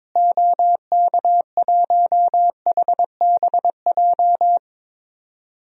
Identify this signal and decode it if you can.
Morse code, heard as a keyed tone, about 22 wpm, 700 Hz. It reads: OK1HBJ
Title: DXCC-100 Call Signs - 6 Characters - Repeated 3x - 22wpm